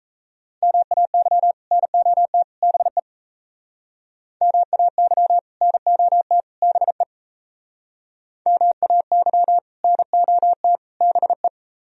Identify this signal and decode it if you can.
Morse code, heard as a keyed tone, about 34 wpm, 700 Hz. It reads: MAY NOT BE MAY NOT BE MAY NOT BE
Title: Speed-Racing - Sets of 3 Words 28wpm